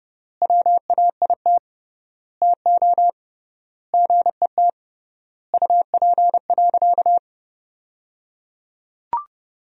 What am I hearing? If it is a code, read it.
Morse code, heard as a keyed tone, about 29 wpm, 700 Hz. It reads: WAIT TO GET UP.
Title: 3x Word-Spacing - Sentences from Top 500 Words - 30wpm